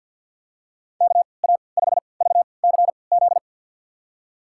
Morse code, heard as a keyed tone, about 52 wpm, spaced as for 29 wpm, 700 Hz. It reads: KA5VCZ
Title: USA Call Signs - 6 Characters - 3x Character Spacing - 50wpm